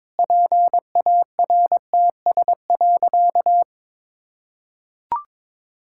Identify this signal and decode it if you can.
Morse code, heard as a keyed tone, about 22 wpm, 700 Hz. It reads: PARTS.
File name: Sentences with 7-Word Phrases - 22wpm